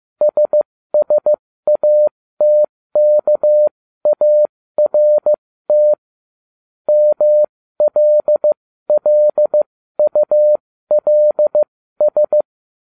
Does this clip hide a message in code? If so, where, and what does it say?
Yes: Morse code from 0.2 to 12.4 s, heard as a keyed tone, about 15 wpm, spaced as for 13 wpm, 600 Hz. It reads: SSATKART MLLULS